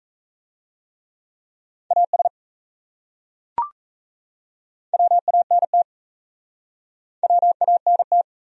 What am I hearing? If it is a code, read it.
Morse code, heard as a keyed tone, about 40 wpm, 700 Hz. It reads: AS WANT WANT